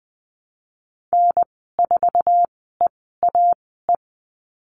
Morse code, heard as a keyed tone, about 20 wpm, spaced as for 15 wpm, 700 Hz. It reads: N4EAE
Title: 088 Lesson 15g - Number 5 - Call Signs 20wpm